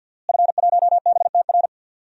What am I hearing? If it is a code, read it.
Morse code, heard as a keyed tone, about 49 wpm, 700 Hz. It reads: F1BTR
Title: DXCC-100 Call Signs - 5 Characters - Repeated 3x - 50wpm